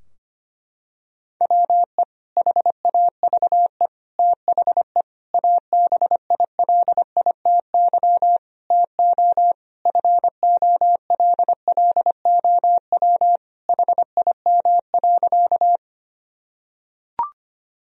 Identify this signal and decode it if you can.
Morse code, heard as a keyed tone, about 25 wpm, 700 Hz. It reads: WE HAVE THE ABILITY TO FOLLOW HIM.
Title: Sentences from Top 1000 Words - Review Part 3 - 25wpm